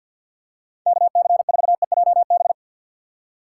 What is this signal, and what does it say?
Morse code, heard as a keyed tone, about 52 wpm, 700 Hz. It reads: KC4EJB
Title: USA Call Signs - 2x Character Spacing - 50wpm